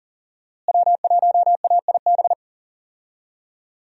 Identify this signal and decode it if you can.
Morse code, heard as a keyed tone, about 39 wpm, 700 Hz. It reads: W1AIB
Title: USA Call Signs - 5 Characters - Repeated 2x - 40wpm